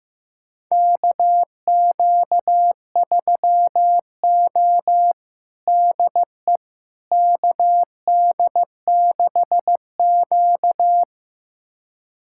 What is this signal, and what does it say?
Morse code, heard as a keyed tone, about 15 wpm, 700 Hz. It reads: KQ3O DE KD6Q
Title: USA Callsign DE Callsign - 4 Characters - 15wpm